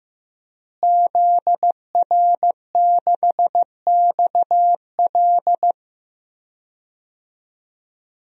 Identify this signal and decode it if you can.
Morse code, heard as a keyed tone, about 15 wpm, 700 Hz. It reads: ZR6XL
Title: DXCC-100 Call Signs - 5 Characters - Repeated 2x - 15wpm